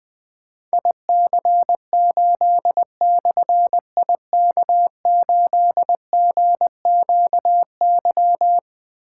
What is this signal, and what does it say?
Morse code, heard as a keyed tone, about 20 wpm, 700 Hz. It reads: IC8/IK8GQY
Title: Difficult Call Signs 20wpm